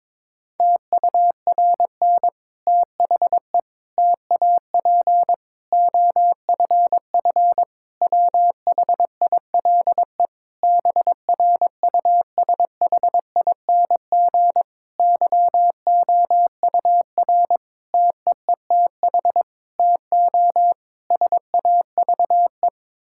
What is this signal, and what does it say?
Morse code, heard as a keyed tone, about 22 wpm, 700 Hz. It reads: TURN THE TAP OFF WHILE BRUSHING YOUR TEETH TO SAVE
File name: Difficult Sentences with Hints - 22wpm